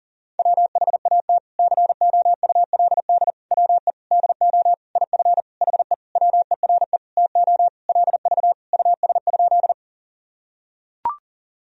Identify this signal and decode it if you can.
Morse code, heard as a keyed tone, about 40 wpm, 700 Hz. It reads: WHAT COULD WE DO IF HE WERE TO LV US?
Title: Sentences from Top 400 Words and Abbreviations 40wpm